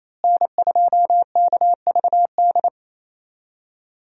Morse code, heard as a keyed tone, about 28 wpm, 700 Hz. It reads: N2KVD
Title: USA Call Signs - 5 Characters - Repeated 2x - 28wpm